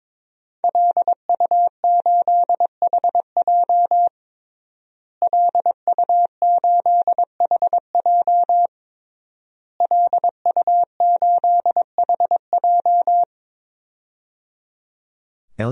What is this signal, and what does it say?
Morse code, heard as a keyed tone, about 22 wpm, 700 Hz. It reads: LU8HJ LU8HJ LU8HJ
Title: DXCC-100 Call Signs - 5 Characters - Repeated 3x - 22wpm